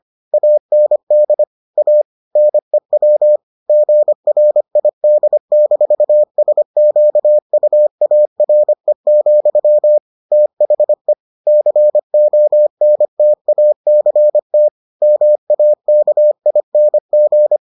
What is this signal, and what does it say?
Morse code, heard as a keyed tone, about 25 wpm, 600 Hz. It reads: AND A NEW GRID-SQUARE, THE CONTACT MAKING